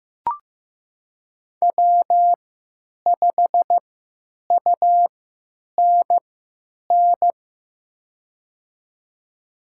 Morse code, heard as a keyed tone, about 15 wpm, spaced as for 9 wpm, 700 Hz. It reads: W5UNN
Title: USA Call Signs - 5 Characters - 3x Character Spacing - 15wpm